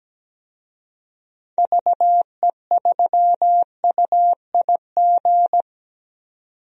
Morse code, heard as a keyed tone, about 17 wpm, 700 Hz. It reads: VE3UIG
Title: DXCC-100 Call Signs - 6 Characters - 17wpm